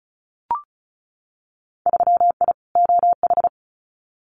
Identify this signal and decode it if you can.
Morse code, heard as a keyed tone, about 36 wpm, 700 Hz. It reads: 3I OH
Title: Winter Field Day Exchanges Encore 35wpm